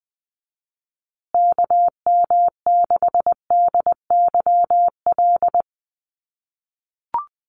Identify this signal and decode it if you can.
Morse code, heard as a keyed tone, about 19 wpm, 700 Hz. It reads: KM6DYL
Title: USA Call Signs Repeated 3x 20wpm